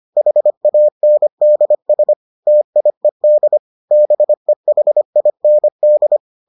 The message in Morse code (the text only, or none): HANDS TIED BEHIND